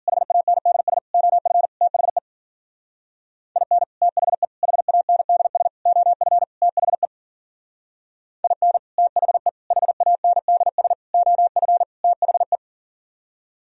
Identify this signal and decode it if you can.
Morse code, heard as a keyed tone, about 48 wpm, 700 Hz. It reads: HANDS OF THE IN THE HANDS OF THE IN THE HANDS OF THE
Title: Speed-Racing - Sets of 5 Words 40wpm